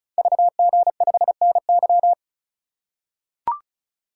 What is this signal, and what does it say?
Morse code, heard as a keyed tone, about 36 wpm, 700 Hz. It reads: VG5NY
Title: Code Groups - 5 Letter:Numbers - 35wpm